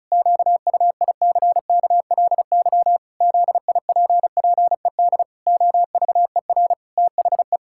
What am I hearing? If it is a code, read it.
Morse code, heard as a keyed tone, about 36 wpm, 700 Hz. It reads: QUICKLY ZIPPED OVER THE
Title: Pangram Challenge v2 - 35wpm